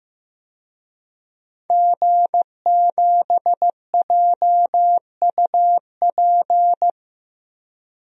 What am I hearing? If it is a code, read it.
Morse code, heard as a keyed tone, about 15 wpm, 700 Hz. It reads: G7JUP
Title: DXCC-100 Call Signs - 5 Characters - Repeated 2x - 15wpm